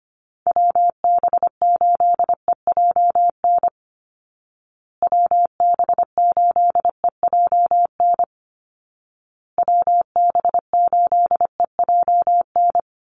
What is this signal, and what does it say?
Morse code, heard as a keyed tone, about 25 wpm, 700 Hz. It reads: WB8EJN WB8EJN WB8EJN
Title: Call Signs Repeated 2x - 25wpm